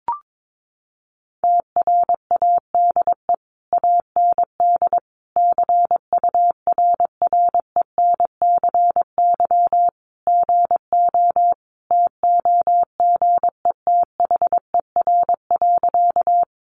Morse code, heard as a keyed tone, about 22 wpm, 700 Hz. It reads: TRADE AND CURRENCY GO TOGETHER.
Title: Sentences from Top 1500 Words 22wpm